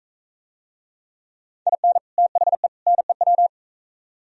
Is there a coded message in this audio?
Yes: Morse code from 1.7 to 3.5 s, heard as a keyed tone, about 43 wpm, 700 Hz. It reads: IN THE NEW